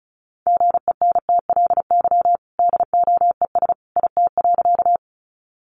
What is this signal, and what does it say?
Morse code, heard as a keyed tone, about 34 wpm, 700 Hz. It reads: GENTLY DOES IT.
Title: Sentences from Top 1500 Words 35wpm